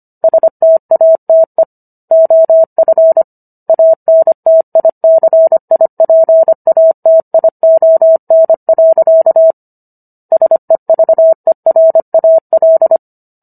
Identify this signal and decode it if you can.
Morse code, heard as a keyed tone, about 25 wpm, 650 Hz. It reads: STATE OF ANTICIPATION. SEVERAL